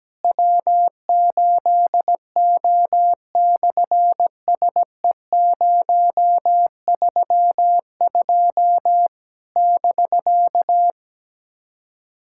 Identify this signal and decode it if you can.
Morse code, heard as a keyed tone, about 17 wpm, 700 Hz. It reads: W8O/SE032 <BK>
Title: SOTA QSO Practice 17wpm